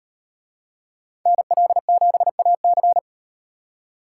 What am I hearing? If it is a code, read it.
Morse code, heard as a keyed tone, about 38 wpm, 700 Hz. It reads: NL7AC